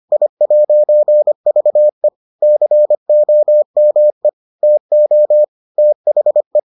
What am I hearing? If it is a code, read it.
Morse code, heard as a keyed tone, about 25 wpm, 600 Hz. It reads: I'VE COME TO THE